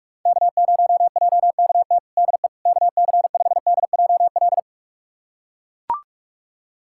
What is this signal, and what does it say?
Morse code, heard as a keyed tone, about 42 wpm, 690 Hz. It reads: K0JKT DE KC5DJL
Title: USA Callsign DE Callsign 45wpm